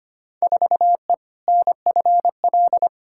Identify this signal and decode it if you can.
Morse code, heard as a keyed tone, about 25 wpm, 700 Hz. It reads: 4E NFL
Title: Field Day Exchanges Encore - 25wpm